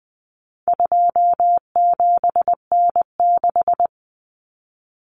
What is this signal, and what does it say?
Morse code, heard as a keyed tone, about 20 wpm, 700 Hz. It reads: 27N6
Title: Common Cut Numbers - 4-Digit 20wpm